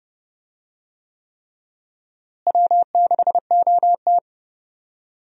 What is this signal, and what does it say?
Morse code, heard as a keyed tone, about 30 wpm, 700 Hz. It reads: W6OT
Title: USA Call Signs - 4 Characters - Repeated 3x - 30wpm